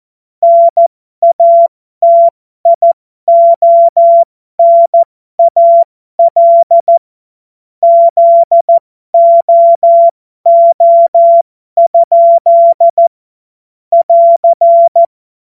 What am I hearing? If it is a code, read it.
Morse code, heard as a keyed tone, about 14 wpm, spaced as for 12 wpm, 680 Hz. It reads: NATIONAL ZOO? <AR>